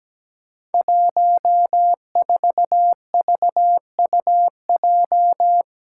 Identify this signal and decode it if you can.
Morse code, heard as a keyed tone, about 17 wpm, 700 Hz. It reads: 14VUJ
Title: Five Challenging Letter-Number Combinations 17wpm